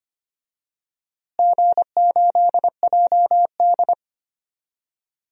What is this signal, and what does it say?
Morse code, heard as a keyed tone, about 25 wpm, 700 Hz. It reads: G8JD
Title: DXCC-100 Call Signs - 4 Characters - Repeated 2x - 25wpm